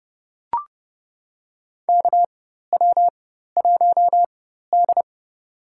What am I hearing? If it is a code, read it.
Morse code, heard as a keyed tone, about 32 wpm, spaced as for 14 wpm, 700 Hz. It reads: KW1D
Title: USA Call Signs - 4 Characters - 4x Character Spacing - 30wpm